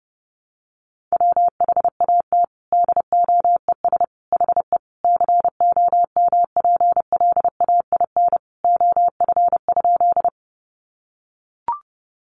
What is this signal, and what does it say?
Morse code, heard as a keyed tone, about 28 wpm, 700 Hz. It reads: WHAT DOES HE COMPLAIN OF?
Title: Sentences from Top 1300 Words 30wpm